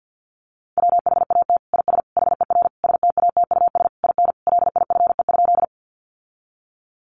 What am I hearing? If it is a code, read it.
Morse code, heard as a keyed tone, about 50 wpm, 700 Hz. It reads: WHAT IS HER STATUS IN LIFE?
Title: Sentences from Top 1400 Words 50wpm